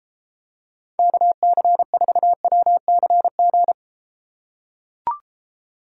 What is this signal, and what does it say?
Morse code, heard as a keyed tone, about 31 wpm, 700 Hz. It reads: KC4WCG